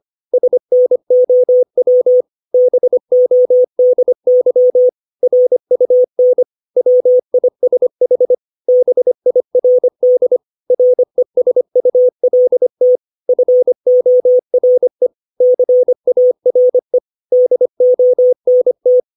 Morse code, heard as a keyed tone, about 25 wpm, 500 Hz. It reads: SNOW BODY RUN WISH BIRD RESULT FORE CARE DONT
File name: 632 words / 632 Words 25WPM